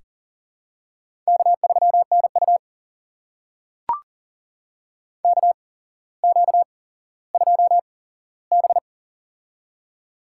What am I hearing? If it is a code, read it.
Morse code, heard as a keyed tone, about 38 wpm, spaced as for 17 wpm, 700 Hz. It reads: K3NU KQ2B
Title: USA Call Signs - 4 Characters - 8x Character Spacing - 40wpm